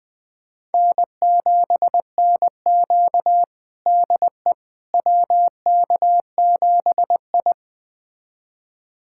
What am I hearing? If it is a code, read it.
Morse code, heard as a keyed tone, about 20 wpm, 700 Hz. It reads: N7NQ DE WK7I
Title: USA Callsign DE Callsign - 4 Characters - 20wpm